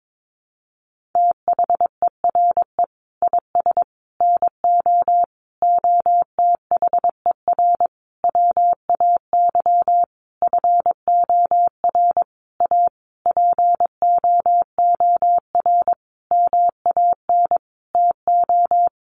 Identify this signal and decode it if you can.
Morse code, heard as a keyed tone, about 22 wpm, 700 Hz. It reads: THERE IS NO OTHER WAY FOR A POOR MAN TO
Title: Sentences from Top 1300 Words 22wpm